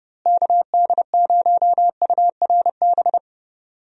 Morse code, heard as a keyed tone, about 31 wpm, 700 Hz. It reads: KD0URB
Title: USA Call Signs - 6 Characters - Repeated 3x - 30wpm